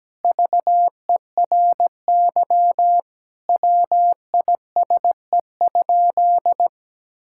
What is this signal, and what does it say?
Morse code, heard as a keyed tone, about 17 wpm, 700 Hz. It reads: VERY WISE?
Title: Sentences from Top 1600 Words 17wpm